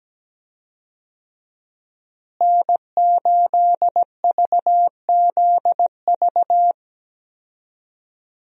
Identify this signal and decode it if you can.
Morse code, heard as a keyed tone, about 17 wpm, 700 Hz. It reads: N8VZV
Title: USA Call Signs - 5 Characters - Repeated 3x - 17wpm